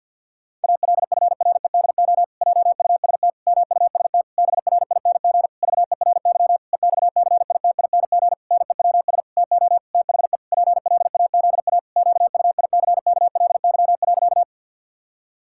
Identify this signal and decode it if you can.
Morse code, heard as a keyed tone, about 50 wpm, 700 Hz. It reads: ALFREDO JUST MUST BRING VERY EXCITING NEWS TO THE PLAZA QUICKLY.
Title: Pangram Challenge v2 - 50wpm